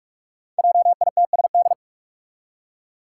Morse code, heard as a keyed tone, about 46 wpm, 700 Hz. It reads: JITSD
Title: Five Challenging Letter-Number Combinations 45wpm